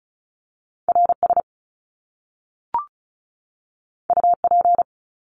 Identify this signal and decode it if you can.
Morse code, heard as a keyed tone, about 36 wpm, 700 Hz. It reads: RS UP
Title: Top Two Letter Combinations in Words 35wpm